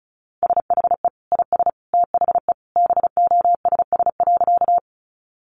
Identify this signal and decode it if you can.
Morse code, heard as a keyed tone, about 34 wpm, 700 Hz. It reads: SHE IS THE BOSS.
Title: Sentences from Top 700 Words 35wpm